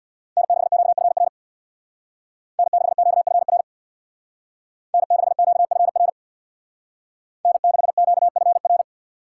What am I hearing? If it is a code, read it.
Morse code, heard as a keyed tone, about 64 wpm, 700 Hz. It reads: N6QFR N6QFR N6QFR N6QFR